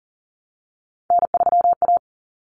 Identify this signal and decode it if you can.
Morse code, heard as a keyed tone, about 40 wpm, 700 Hz. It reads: N3A
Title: Common Cut Numbers - 3-Digit 40wpm